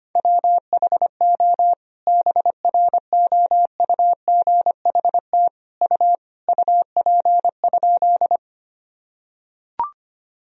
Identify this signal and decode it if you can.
Morse code, heard as a keyed tone, about 25 wpm, 700 Hz. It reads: WHO BROUGHT U UP?
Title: Sentences from Top 500 Words and Abbreviations 25wpm